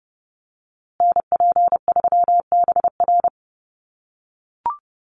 Morse code, heard as a keyed tone, about 30 wpm, 700 Hz. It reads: NP3BR
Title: USA Call Signs Repeated 2x 30wpm